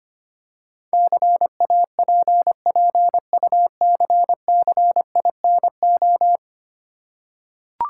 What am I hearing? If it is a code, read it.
Morse code, heard as a keyed tone, about 25 wpm, 690 Hz. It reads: CAPPUCCINO
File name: practice / 25wpm / Easy Words with Hints - 25wpm